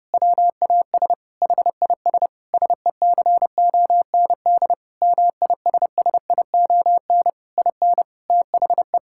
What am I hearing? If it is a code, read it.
Morse code, heard as a keyed tone, about 30 wpm, 700 Hz. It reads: WAS HIS SECOND MISSION IN THE
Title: Sentences from Top 1600 Words 30wpm